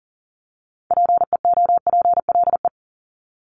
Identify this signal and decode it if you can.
Morse code, heard as a keyed tone, about 38 wpm, 700 Hz. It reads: PEOPLE
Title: Multi-letter Phonemes 40wpm